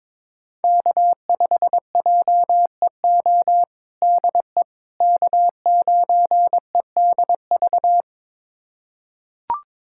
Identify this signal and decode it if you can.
Morse code, heard as a keyed tone, about 22 wpm, 700 Hz. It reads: K5JEO DE K9EDV
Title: USA Callsign DE Callsign - 5 Characters - 22wpm